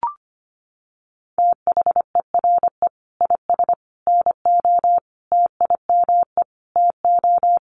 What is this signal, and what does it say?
Morse code, heard as a keyed tone, about 25 wpm, 700 Hz. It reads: THERE IS NO TIME TO
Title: Sentences from Top 1100 Words 25wpm